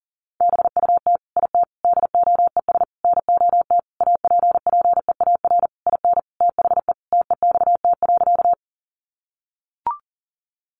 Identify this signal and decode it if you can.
Morse code, heard as a keyed tone, about 40 wpm, 700 Hz. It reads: BUT IT DOES NOT APPEAR IN THE TEXT.
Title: Sentences from Top 1200 Words 40wpm